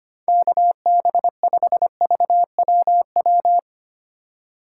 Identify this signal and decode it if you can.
Morse code, heard as a keyed tone, about 25 wpm, 700 Hz. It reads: KB5VWW
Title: USA Call Signs - 2x Character Spacing - 25wpm